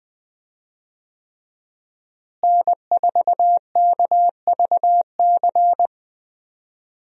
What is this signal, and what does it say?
Morse code, heard as a keyed tone, about 20 wpm, 700 Hz. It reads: N4KVC
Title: USA Call Signs - 5 Characters - Repeated 3x - 20wpm